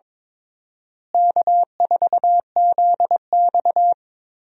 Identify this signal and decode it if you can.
Morse code, heard as a keyed tone, about 22 wpm, 700 Hz. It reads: K4ZX
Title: USA Call Signs - 4 Characters - Repeated 3x - 22wpm